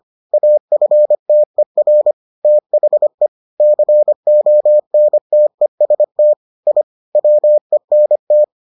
Morse code, heard as a keyed tone, about 25 wpm, 600 Hz. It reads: AFTER THE CONTEST I WENT